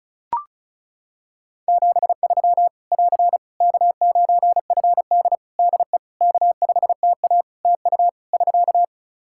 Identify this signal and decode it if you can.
Morse code, heard as a keyed tone, about 36 wpm, 700 Hz. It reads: 73 <AR> K9FD DE K5TA TU <SK>
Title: K1USN SST QSO Practice 35wpm